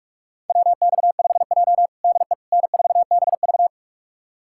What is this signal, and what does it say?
Morse code, heard as a keyed tone, about 47 wpm, 700 Hz. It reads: WX5J DE N4BV
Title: USA Callsign DE Callsign - 4 Characters - 45wpm